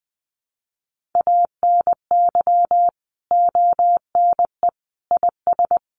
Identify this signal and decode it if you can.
Morse code, heard as a keyed tone, about 20 wpm, 700 Hz. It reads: ANY ONE IS
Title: Sentences from Top 1200 Words 20wpm